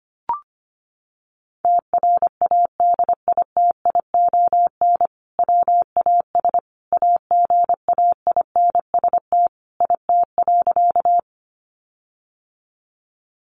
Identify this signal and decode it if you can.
Morse code, heard as a keyed tone, about 25 wpm, 700 Hz. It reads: TRADITION WAS AGAINST IT.
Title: Sentences from Top 1500 Words 25wpm